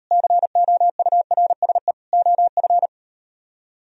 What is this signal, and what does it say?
Morse code, heard as a keyed tone, about 38 wpm, 700 Hz. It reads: COURSE OF